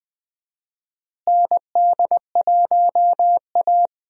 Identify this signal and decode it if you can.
Morse code, heard as a keyed tone, about 20 wpm, 700 Hz. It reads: ND1A
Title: USA Call Signs - 4 Characters - Repeated 3x - 20wpm